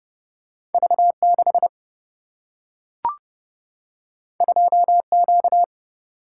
Morse code, heard as a keyed tone, about 30 wpm, 700 Hz. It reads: V 6 2 Q
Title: Single-Letter Single-Number 30wpm